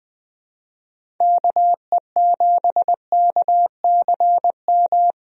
Morse code, heard as a keyed tone, about 20 wpm, 700 Hz. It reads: KE7KCM
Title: USA Call Signs - 6 Characters - 20wpm